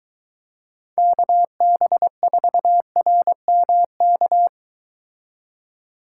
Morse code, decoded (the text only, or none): KB4RMK